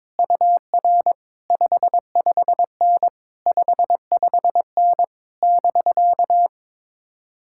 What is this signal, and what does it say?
Morse code, heard as a keyed tone, about 22 wpm, 700 Hz. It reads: UR 55N 55N <BK>
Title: Speed-Racing Common QSO Phrases - 22wpm